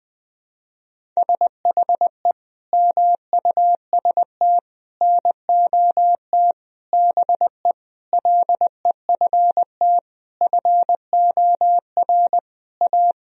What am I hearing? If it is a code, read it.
Morse code, heard as a keyed tone, about 21 wpm, 700 Hz. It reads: SHE MUST NOT BE LEFT FOR A
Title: Sentences from Top 1000 Words - Review Part 1 - 20wpm